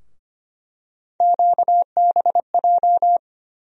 Morse code, heard as a keyed tone, about 25 wpm, 700 Hz. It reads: QBJ
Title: USA Call Sign Postfixes - 3 Letters 25wpm